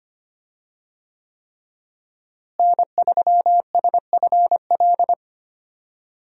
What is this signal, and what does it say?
Morse code, heard as a keyed tone, about 25 wpm, 700 Hz. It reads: N3SFL
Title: USA Call Signs - 5 Characters - Repeated 3x - 25wpm